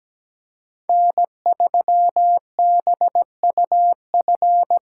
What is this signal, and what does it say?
Morse code, heard as a keyed tone, about 20 wpm, 700 Hz. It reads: N3BUF
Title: Speed Racing - USA Call Signs 17wpm